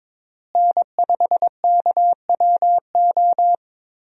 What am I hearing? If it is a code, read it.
Morse code, heard as a keyed tone, about 22 wpm, 700 Hz. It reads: N5KWO
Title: USA Call Signs - 5 Characters - Repeated 3x - 22wpm